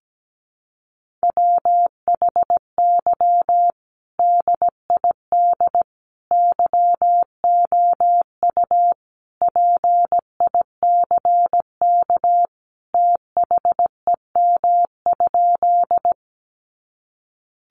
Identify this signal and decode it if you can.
Morse code, heard as a keyed tone, about 17 wpm, 700 Hz. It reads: WHY DID YOU PICK THEM?
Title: Sentences from Top 1000 Words 17wpm